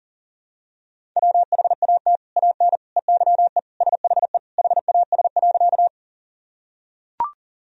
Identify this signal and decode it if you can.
Morse code, heard as a keyed tone, about 37 wpm, 700 Hz. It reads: WHAT AN EYE SHE HAS.
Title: Sentences from Top 1000 Words - Review Part 1 - 40wpm